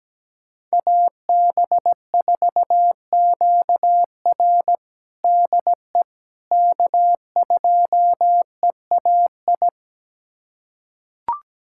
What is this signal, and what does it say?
Morse code, heard as a keyed tone, about 17 wpm, 700 Hz. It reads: AB4QR DE K2EAI
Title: USA Callsign DE Callsign - 5 Characters - 17wpm